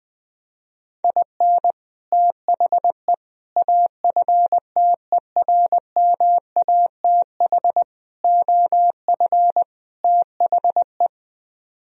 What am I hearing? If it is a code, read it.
Morse code, heard as a keyed tone, about 20 wpm, 700 Hz. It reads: IN THE AFTERMATH OF THE
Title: Speed-Racing - Sets of 5 Words 17wpm